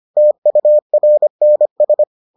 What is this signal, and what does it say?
Morse code, heard as a keyed tone, about 25 wpm, 600 Hz. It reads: TURNS